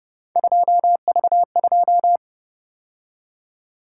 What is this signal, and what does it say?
Morse code, heard as a keyed tone, about 30 wpm, 700 Hz. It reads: 2V2